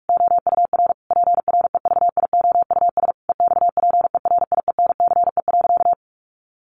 Morse code, heard as a keyed tone, about 44 wpm, 700 Hz. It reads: OUR PREVIOUS EXPERIENCE.
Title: Sentences from Top 1500 Words 45wpm